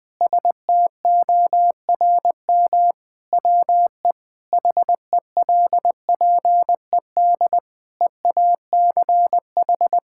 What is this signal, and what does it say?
Morse code, heard as a keyed tone, about 20 wpm, 700 Hz. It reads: STORM WE HELPED EACH
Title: Sentences with 5-Word Phrases - 20wpm